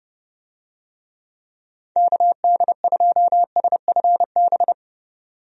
Morse code, heard as a keyed tone, about 30 wpm, 700 Hz. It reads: KD2SFB
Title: USA Call Signs - 6 Characters - 30wpm